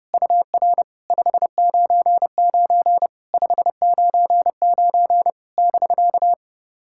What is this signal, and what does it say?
Morse code, heard as a keyed tone, about 28 wpm, 660 Hz. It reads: UR 599 599 <BK>
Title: SOTA QSO Practice 30wpm